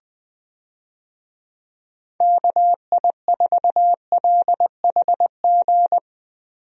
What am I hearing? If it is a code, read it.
Morse code, heard as a keyed tone, about 20 wpm, 700 Hz. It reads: KI4LHG
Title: USA Call Signs - 6 Characters - Repeated 2x - 20wpm